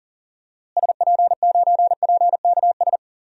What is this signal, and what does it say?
Morse code, heard as a keyed tone, about 38 wpm, 700 Hz. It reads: SP9PKS
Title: DXCC-100 Call Signs - 6 Characters - Repeated 2x - 40wpm